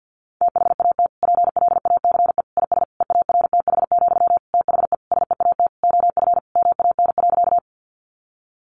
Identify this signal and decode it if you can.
Morse code, heard as a keyed tone, about 50 wpm, 700 Hz. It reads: THAT PLACE IS EARTH, THE SEAT OF MAN.
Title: Sentences from Top 1500 Words 50wpm